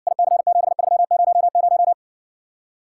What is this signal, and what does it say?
Morse code, heard as a keyed tone, about 60 wpm, 700 Hz. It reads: IC7300